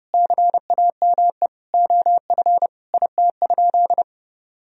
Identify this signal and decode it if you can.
Morse code, heard as a keyed tone, about 30 wpm, 700 Hz. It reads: CAME OF IT?
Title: Sentences from Top 1000 Words - Review Part 3 - 30wpm